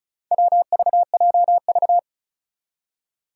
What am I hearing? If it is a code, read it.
Morse code, heard as a keyed tone, about 36 wpm, 700 Hz. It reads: WVJV
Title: Four Challenging Letter-Number Combinations 35wpm